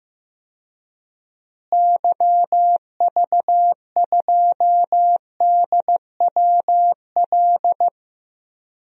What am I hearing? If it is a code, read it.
Morse code, heard as a keyed tone, about 16 wpm, 700 Hz. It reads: YV2DWL
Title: DXCC-100 Call Signs - 6 Characters - Repeated 3x - 15wpm